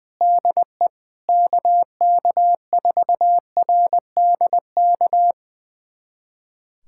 Morse code, heard as a keyed tone, about 20 wpm, 700 Hz. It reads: DE KK4RDK
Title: USA Callsign DE Callsign 20wpm